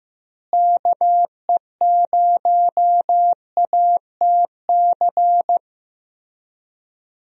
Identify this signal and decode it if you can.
Morse code, heard as a keyed tone, about 15 wpm, 700 Hz. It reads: KE0ATC
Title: USA Call Signs - 6 Characters - Repeated 3x - 15wpm